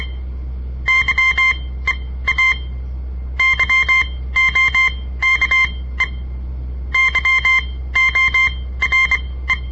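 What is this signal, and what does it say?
Morse code, heard as a keyed tone, about 23 wpm, spaced as for 16 wpm, 2100 Hz. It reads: YEA YOKE YORE
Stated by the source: On-air Morse code practice broadcast by a southern Montana amateur radio association; all code, no read back.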